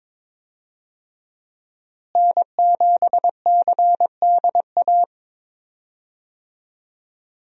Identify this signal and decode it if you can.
Morse code, heard as a keyed tone, about 21 wpm, 700 Hz. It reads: N7CDA
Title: USA Call Signs - 5 Characters - Repeated 2x - 22wpm